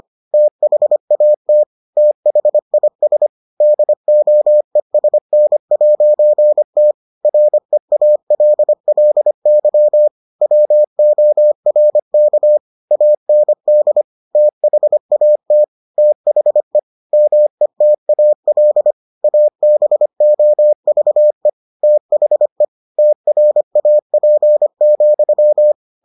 Morse code, heard as a keyed tone, about 25 wpm, 600 Hz. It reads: THAT THIS DOESN'T REALLY WORK AND THAT THE METAL ABOVE THE TRAP,